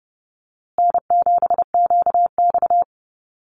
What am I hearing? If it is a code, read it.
Morse code, heard as a keyed tone, about 30 wpm, 700 Hz. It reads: N7QX
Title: USA Call Signs Repeated 3x 30wpm